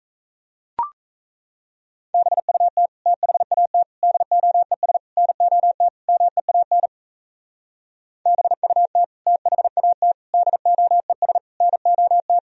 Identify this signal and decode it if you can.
Morse code, heard as a keyed tone, about 35 wpm, 700 Hz. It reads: BUT THAT DOES NOT MEAN BUT THAT DOES NOT
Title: Speed-Racing - Sets of 5 Words 28wpm